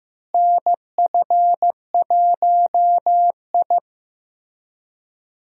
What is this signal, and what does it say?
Morse code, heard as a keyed tone, about 15 wpm, 700 Hz. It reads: NF1I
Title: USA Call Signs - 4 Characters - Repeated 2x - 15wpm